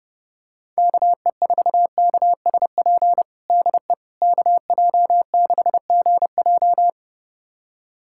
Morse code, heard as a keyed tone, about 32 wpm, 700 Hz. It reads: KE4KSP DE KJ6GJ
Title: USA Callsign DE Callsign 30wpm